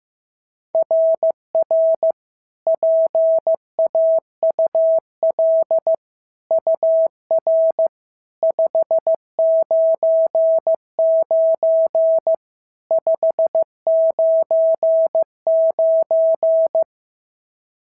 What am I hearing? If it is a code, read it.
Morse code, heard as a keyed tone, about 14 wpm, 650 Hz. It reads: RR PAUL UR 599 599
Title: SOTA QSO Practice 15wpm